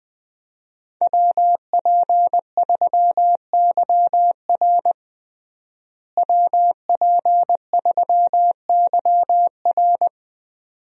Morse code, heard as a keyed tone, about 20 wpm, 700 Hz. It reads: WP3YR WP3YR
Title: USA Call Signs - 5 Characters - Repeated 2x - 20wpm